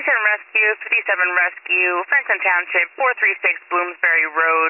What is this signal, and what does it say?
Emergency Alert System (SAME) tones, heard as harsh buzzing bursts, with speech. It is heard from 1.1 to 4.4 s.